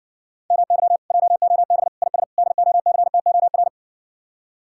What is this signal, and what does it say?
Morse code, heard as a keyed tone, about 60 wpm, 700 Hz. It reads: MY JOB IS DOCTOR